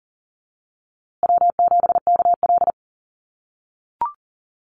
Morse code, heard as a keyed tone, about 40 wpm, 700 Hz. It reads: W7KL
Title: USA Call Signs Repeated 2x 40wpm